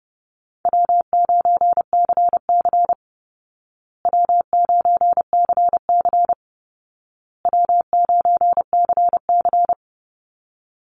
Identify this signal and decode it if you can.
Morse code, heard as a keyed tone, about 30 wpm, 700 Hz. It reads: W9CC W9CC W9CC
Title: Call Signs Repeated 2x - 30wpm